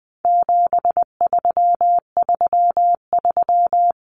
Morse code, heard as a keyed tone, about 20 wpm, 700 Hz. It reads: 7333
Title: Numbers - 4-Digit 20wpm